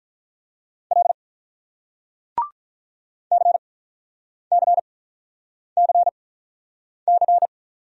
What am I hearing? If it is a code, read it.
Morse code, heard as a keyed tone, about 47 wpm, spaced as for 9 wpm, 700 Hz. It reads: L CCCC